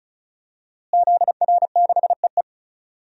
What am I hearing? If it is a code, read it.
Morse code, heard as a keyed tone, about 36 wpm, 700 Hz. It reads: ZR6EE
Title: DXCC-100 Call Signs - 5 Characters - Repeated 2x - 35wpm